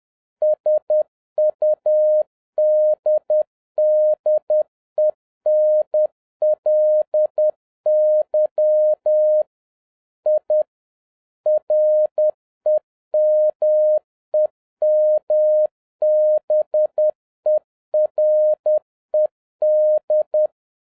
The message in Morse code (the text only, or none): SUDDENLY I REMEMBERED